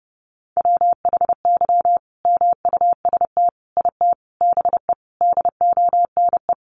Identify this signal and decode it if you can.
Morse code, heard as a keyed tone, about 30 wpm, 700 Hz. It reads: WHY MUST IT BE DONE
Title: Sentences from Top 600 Words 30wpm